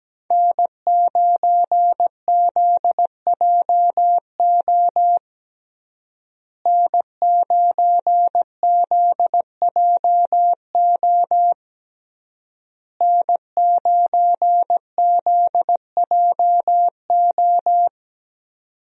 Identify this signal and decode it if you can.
Morse code, heard as a keyed tone, about 17 wpm, 700 Hz. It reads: N9ZJO N9ZJO N9ZJO
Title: USA Call Signs - 5 Characters - Repeated 3x - 17wpm